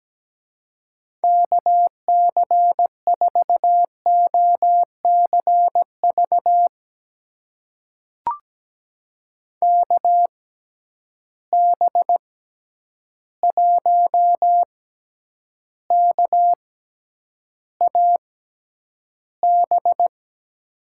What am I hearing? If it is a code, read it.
Morse code, heard as a keyed tone, about 17 wpm, spaced as for 9 wpm, 700 Hz. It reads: KC4OCV KB1KAB
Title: USA Call Signs - 6 Characters - 6x Character Spacing - 17wpm